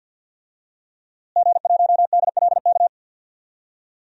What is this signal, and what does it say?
Morse code, heard as a keyed tone, about 51 wpm, 700 Hz. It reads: G1DLK
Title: DXCC-100 Call Signs - 5 Characters - Repeated 2x - 50wpm